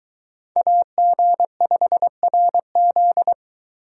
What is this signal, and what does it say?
Morse code, heard as a keyed tone, about 22 wpm, 700 Hz. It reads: AG5RZ